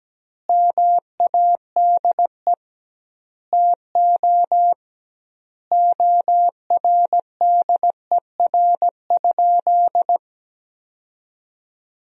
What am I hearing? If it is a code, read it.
Morse code, heard as a keyed tone, about 17 wpm, 700 Hz. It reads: MADE TO ORDER?
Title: 2x Word-Spacing - Sentences from Top 400 Words - 17wpm